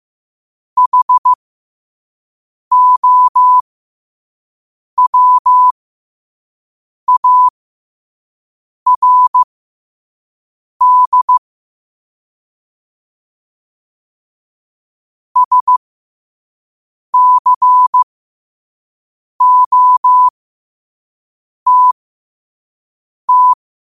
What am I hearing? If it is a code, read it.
Morse code, heard as a keyed tone, about 15 wpm, spaced as for 5 wpm, 1000 Hz. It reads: HOWARD SCOTT
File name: qotdfiles / 5wpm